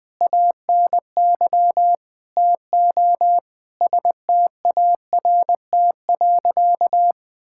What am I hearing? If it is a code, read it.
Morse code, heard as a keyed tone, about 21 wpm, 700 Hz. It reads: ANY TO START.
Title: Sentences with 8-Word Phrases - 20wpm